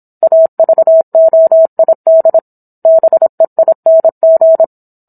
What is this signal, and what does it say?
Morse code, heard as a keyed tone, about 27 wpm, 650 Hz. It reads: AVOID BEING